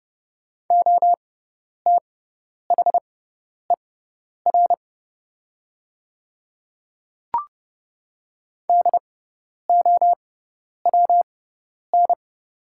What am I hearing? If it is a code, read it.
Morse code, heard as a keyed tone, about 30 wpm, spaced as for 10 wpm, 700 Hz. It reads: OTHER DOWN
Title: 6x Character Spacing - Top 100 Words - 30wpm